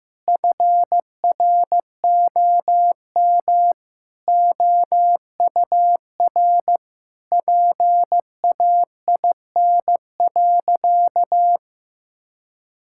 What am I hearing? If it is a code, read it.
Morse code, heard as a keyed tone, about 16 wpm, 700 Hz. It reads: FROM OUR PAIN.
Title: Sentences from Top 1600 Words 15wpm